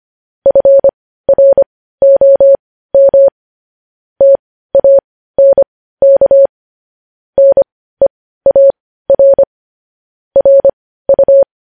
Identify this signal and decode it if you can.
Morse code, heard as a keyed tone, about 25 wpm, spaced as for 15 wpm, 550 Hz. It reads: FROM TANK NEAR RU
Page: news headlines in morse code at 15 wpm